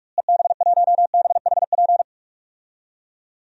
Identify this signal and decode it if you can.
Morse code, heard as a keyed tone, about 41 wpm, 700 Hz. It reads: EB1BHP
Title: DXCC-100 Call Signs - 6 Characters - Repeated 2x - 45wpm